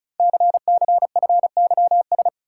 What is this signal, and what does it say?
Morse code, heard as a keyed tone, about 36 wpm, 700 Hz. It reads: CCFYS
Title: Code Groups - 5 Letters - 35wpm